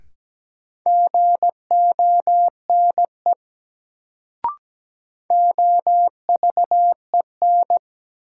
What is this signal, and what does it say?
Morse code, heard as a keyed tone, about 17 wpm, 700 Hz. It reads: GONE OVEN
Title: Sentences from Top 1200 Words - New Word Drill Alt - 17wpm